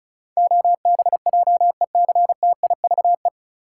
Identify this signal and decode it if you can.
Morse code, heard as a keyed tone, about 34 wpm, 700 Hz. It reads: OBJECTIVE
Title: Sentences from Top 1200 Words - New Word Drill - 35wpm